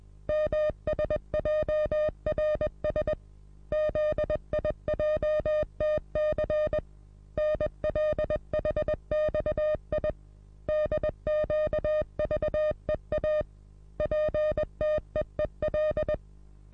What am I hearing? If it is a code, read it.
Morse code, heard as a keyed tone, about 21 wpm, 610 Hz. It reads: MSJRS ZIJTC NLHXI DQVEA PTEEL
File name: repos / Morsekurs / DJ4UF Morsen26